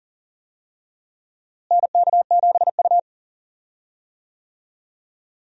Morse code, heard as a keyed tone, about 40 wpm, 700 Hz. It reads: NK7U
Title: USA Call Signs - 4 Characters - Repeated 3x - 40wpm